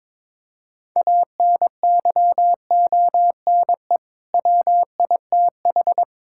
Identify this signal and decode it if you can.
Morse code, heard as a keyed tone, about 22 wpm, 700 Hz. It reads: ANYONE WITH
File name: Sentences with 7-Word Phrases - 22wpm